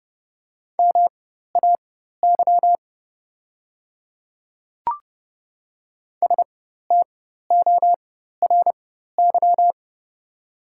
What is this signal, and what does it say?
Morse code, heard as a keyed tone, about 30 wpm, spaced as for 14 wpm, 700 Hz. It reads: MAY STORY
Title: 4x Character Spacing - Top 300 Words - 30wpm